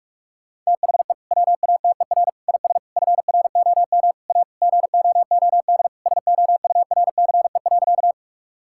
Morse code, heard as a keyed tone, about 41 wpm, 700 Hz. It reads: THE WATER IS FROM A GOOD SOURCE.
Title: Sentences from Top 1000 Words - Review Part 5 - 45wpm